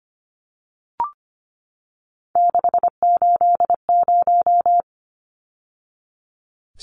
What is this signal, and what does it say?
Morse code, heard as a keyed tone, about 25 wpm, 700 Hz. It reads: 680
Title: Numbers - 3-Digit 25wpm